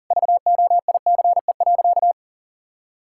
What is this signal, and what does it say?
Morse code, heard as a keyed tone, about 40 wpm, 700 Hz. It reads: VOICE.
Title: Sentences from Top 1600 Words 40wpm